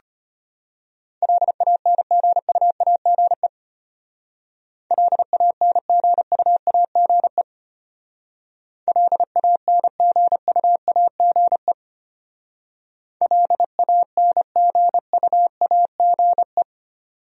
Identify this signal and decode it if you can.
Morse code, heard as a keyed tone, about 32 wpm, 700 Hz. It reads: LANGUAGE LANGUAGE LANGUAGE LANGUAGE